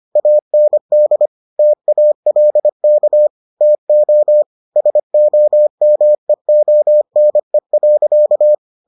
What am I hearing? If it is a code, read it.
Morse code, heard as a keyed tone, about 25 wpm, 600 Hz. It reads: AND TALK TO SOMEONE.